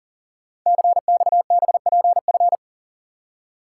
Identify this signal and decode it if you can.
Morse code, heard as a keyed tone, about 38 wpm, 690 Hz. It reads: CXBPF
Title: Code Groups - 5 Letters - 40wpm